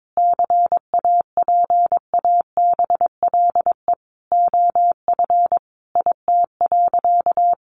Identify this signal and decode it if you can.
Morse code, heard as a keyed tone, about 22 wpm, 700 Hz. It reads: CAPABLE OF IT.
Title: Sentences from Top 1200 Words 22wpm